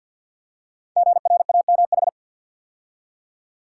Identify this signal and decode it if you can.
Morse code, heard as a keyed tone, about 46 wpm, 700 Hz. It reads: GRAMH
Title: Code Groups - 5 Letters - 50wpm